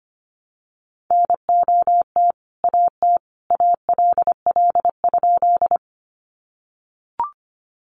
Morse code, heard as a keyed tone, about 25 wpm, 700 Hz. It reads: NOT AT ALL?
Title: Sentences from Top 600 Words 25wpm